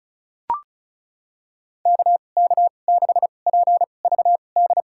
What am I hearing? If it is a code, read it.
Morse code, heard as a keyed tone, about 34 wpm, spaced as for 25 wpm, 700 Hz. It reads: KK6PVD
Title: USA Call Signs - 6 Characters - 2x Character Spacing - 35wpm